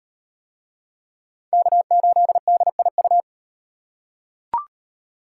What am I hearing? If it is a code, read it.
Morse code, heard as a keyed tone, about 38 wpm, 700 Hz. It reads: K8DIU